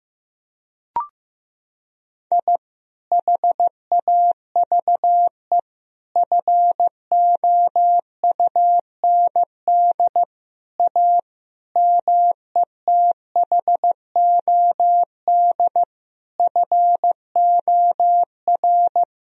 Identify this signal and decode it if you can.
Morse code, heard as a keyed tone, about 15 wpm, 700 Hz. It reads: I HAVE FOUND A METHOD FOR
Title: Sentences from Top 1000 Words - Review Part 3 - 15wpm